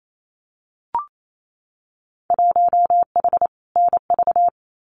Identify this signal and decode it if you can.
Morse code, heard as a keyed tone, about 28 wpm, 700 Hz. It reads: 1H NV
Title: Winter Field Day Exchanges 28wpm